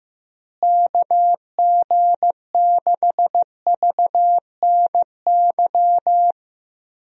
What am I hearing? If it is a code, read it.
Morse code, heard as a keyed tone, about 15 wpm, 700 Hz. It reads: KG6VNY
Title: USA Call Signs - 6 Characters - Repeated 3x - 15wpm